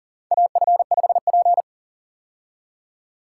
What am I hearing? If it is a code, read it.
Morse code, heard as a keyed tone, about 40 wpm, 700 Hz. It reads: AF5P
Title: K1USN SST QSO Practice - 40wpm